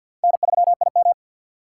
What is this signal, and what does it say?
Morse code, heard as a keyed tone, about 52 wpm, 700 Hz. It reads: N3IM
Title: USA Call Signs - 4 Characters - 3x Character Spacing - 50wpm